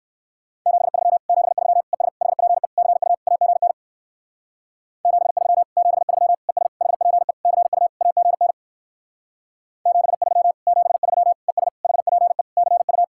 Morse code, heard as a keyed tone, about 56 wpm, 700 Hz. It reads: 73 73 ES HPE CU AGN 73 73 ES HPE CU AGN 73 73 ES HPE CU
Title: Speed-Racing Common QSO Phrases - 45wpm